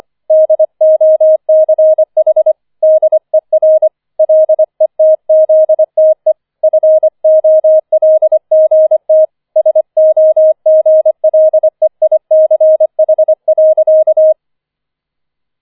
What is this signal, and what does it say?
Morse code, heard as a keyed tone, about 25 wpm, 630 Hz. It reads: DOCH DER LETZTE FOLGT SOGLEICH.
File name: 25wpm